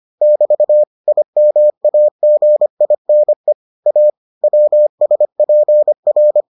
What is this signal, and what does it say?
Morse code, heard as a keyed tone, about 25 wpm, 600 Hz. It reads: <BT> IMAGINE A WSPR